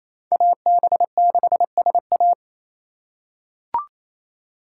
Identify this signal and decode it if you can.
Morse code, heard as a keyed tone, about 28 wpm, 700 Hz. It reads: AB6SA
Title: USA Call Signs - 5 Characters - 2x Character Spacing - 28wpm